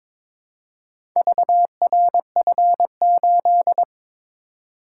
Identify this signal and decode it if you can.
Morse code, heard as a keyed tone, about 22 wpm, 690 Hz. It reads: VRF8
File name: practice / 22wpm / Four Letter-Number Combinations - 22wpm